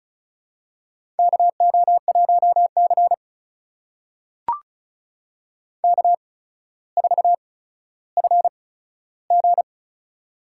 Morse code, heard as a keyed tone, about 36 wpm, spaced as for 15 wpm, 700 Hz. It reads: KO1C K4FG
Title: USA Call Signs - 4 Characters - 8x Character Spacing - 35wpm